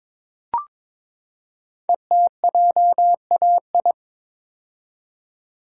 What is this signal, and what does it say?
Morse code, heard as a keyed tone, about 22 wpm, 700 Hz. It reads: ETJAI
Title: Code Groups - 5 Letters - 22wpm